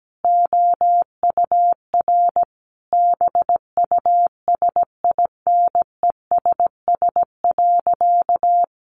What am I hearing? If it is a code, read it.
Morse code, heard as a keyed tone, about 17 wpm, 700 Hz. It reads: OUR BUSINESS.
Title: Sentences from Top 1400 Words 17wpm